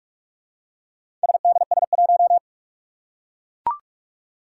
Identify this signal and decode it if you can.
Morse code, heard as a keyed tone, about 44 wpm, 700 Hz. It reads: SDS1
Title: Four Challenging Letter-Number Combinations 45wpm